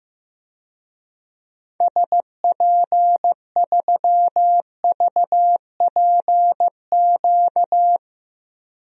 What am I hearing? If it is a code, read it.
Morse code, heard as a keyed tone, about 15 wpm, 700 Hz. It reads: SP3VPQ
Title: DXCC-100 Call Signs - 6 Characters - 15wpm